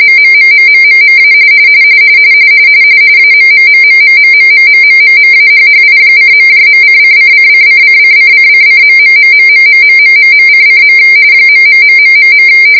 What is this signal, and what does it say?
RTTY, about 45.45 baud. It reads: MMMMMMMA AMMMMMMMMMMMMAHHI:..::::::::::::..........::. ......